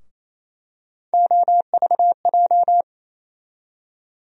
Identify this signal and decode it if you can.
Morse code, heard as a keyed tone, about 28 wpm, 700 Hz. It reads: OVJ
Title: USA Call Sign Postfixes - 3 Letters 28wpm